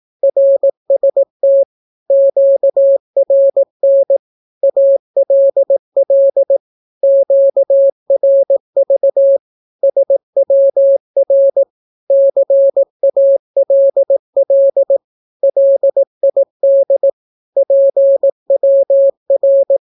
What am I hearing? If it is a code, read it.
Morse code, heard as a keyed tone, about 18 wpm, 550 Hz. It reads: RST QRN ALL QRV SWR CALL LID PWR
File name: naukapodstaw / skroty / skroty